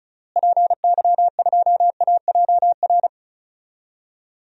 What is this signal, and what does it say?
Morse code, heard as a keyed tone, about 36 wpm, 700 Hz. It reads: PY2AJR
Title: DXCC-100 Call Signs - 6 Characters - Repeated 3x - 35wpm